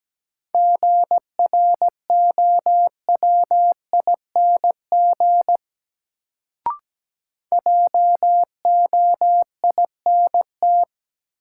Morse code, heard as a keyed tone, about 17 wpm, 700 Hz. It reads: GROWING JOINT
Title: Sets of 5 Words Encore - New Word Drill - 17wpm